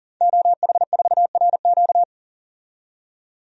Morse code, heard as a keyed tone, about 40 wpm, 700 Hz. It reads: OH4RQ
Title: DXCC-100 Call Signs - 5 Characters - 40wpm